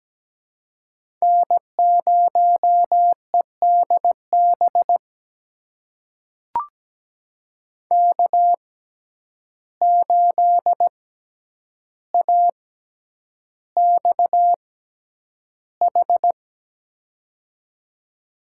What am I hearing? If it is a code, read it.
Morse code, heard as a keyed tone, about 17 wpm, spaced as for 9 wpm, 700 Hz. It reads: N0EDB K8AXH
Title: USA Call Signs - 5 Characters - 6x Character Spacing - 17wpm